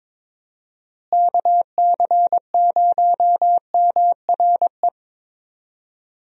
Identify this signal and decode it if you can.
Morse code, heard as a keyed tone, about 22 wpm, 700 Hz. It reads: KC0MRE
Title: USA Call Signs - 6 Characters - Repeated 2x - 22wpm